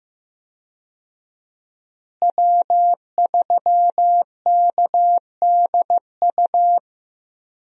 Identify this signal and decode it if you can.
Morse code, heard as a keyed tone, about 15 wpm, 700 Hz. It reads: W3KDU
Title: USA Call Signs - 5 Characters - Repeated 3x - 15wpm